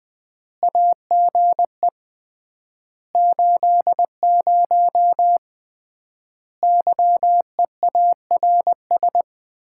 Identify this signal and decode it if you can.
Morse code, heard as a keyed tone, about 20 wpm, 700 Hz. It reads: AGE 80 YEARS
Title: Common QSO Phrases - 3x Word-Spacing - 20wpm